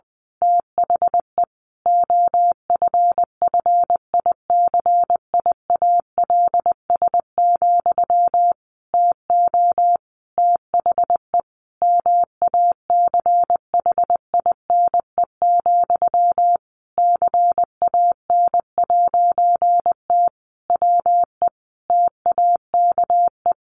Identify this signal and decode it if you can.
Morse code, heard as a keyed tone, about 20 wpm, 700 Hz. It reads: THE OFFICIALS, TO THE MACHINE, CAN'T WE TAKE
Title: A Sound Of Thunder 20wpm